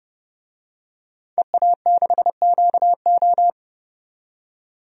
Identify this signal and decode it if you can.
Morse code, heard as a keyed tone, about 30 wpm, 700 Hz. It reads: EA6QO
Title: DXCC-100 Call Signs - 5 Characters - 30wpm